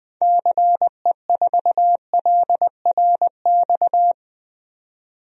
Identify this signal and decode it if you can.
Morse code, heard as a keyed tone, about 21 wpm, 700 Hz. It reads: CE4LRX
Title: DXCC-100 Call Signs - 6 Characters - 20wpm